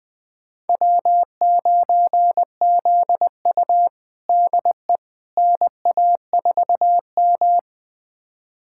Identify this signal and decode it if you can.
Morse code, heard as a keyed tone, about 21 wpm, 700 Hz. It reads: W9ZU DE NA4M
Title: USA Callsign DE Callsign - 4 Characters - 20wpm